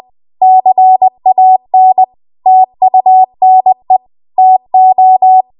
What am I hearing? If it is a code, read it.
Morse code, heard as a keyed tone, about 20 wpm, 750 Hz. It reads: CAN TUNE TO
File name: Archive / 20 WPM / 20WPM